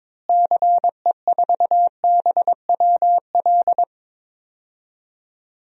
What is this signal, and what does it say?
Morse code, heard as a keyed tone, about 22 wpm, 700 Hz. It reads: CE4BWL
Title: DXCC-100 Call Signs - 6 Characters - Repeated 2x - 22wpm